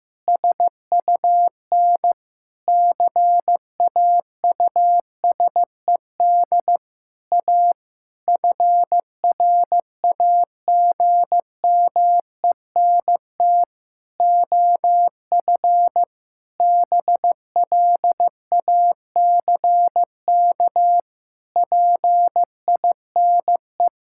Morse code, heard as a keyed tone, about 15 wpm, 700 Hz. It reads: SUN CAUSED A FRAGMENT OF BLACK PINE
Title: Pangram Challenge v2 - 15wpm